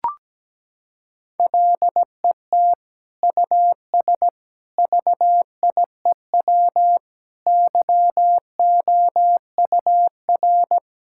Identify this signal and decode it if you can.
Morse code, heard as a keyed tone, about 17 wpm, 700 Hz. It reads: LET US VIEW YOUR
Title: Sentences from Top 1000 Words - Review Part 6 - 17wpm